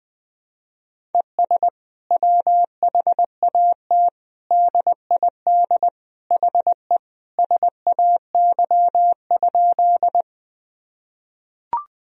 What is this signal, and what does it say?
Morse code, heard as a keyed tone, about 20 wpm, 700 Hz. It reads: ES WHAT DID HE SAY?
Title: Sentences from Top 200 Words and Abbreviations 20wpm